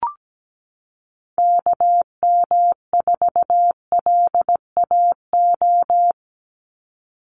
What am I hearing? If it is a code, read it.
Morse code, heard as a keyed tone, about 17 wpm, 700 Hz. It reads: KM4LAO
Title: Call Signs Repeated 3x - 17wpm